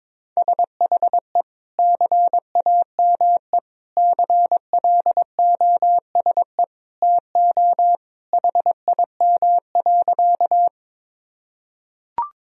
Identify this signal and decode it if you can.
Morse code, heard as a keyed tone, about 22 wpm, 700 Hz. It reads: SHE CAME CLOSE TO HIM.
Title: Sentences from Top 1000 Words - Review Part 1 - 22wpm